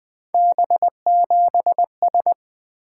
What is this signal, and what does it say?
Morse code, heard as a keyed tone, about 20 wpm, 700 Hz. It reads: B7S
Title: Three Letter-Number Combinations - 20wpm